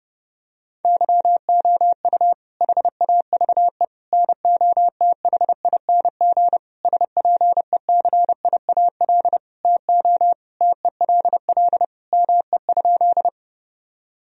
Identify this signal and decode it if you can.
Morse code, heard as a keyed tone, about 30 wpm, 700 Hz. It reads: YOU HAVE NOTHING SPECIAL TO TELL ME?
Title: Sentences from Top 1000 Words - Review Part 3 - 30wpm